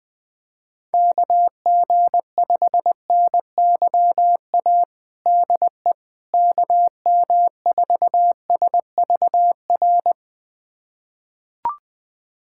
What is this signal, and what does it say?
Morse code, heard as a keyed tone, about 21 wpm, 700 Hz. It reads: KG5NYA DE KM4SVR
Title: USA Callsign DE Callsign - 6 Characters - 20wpm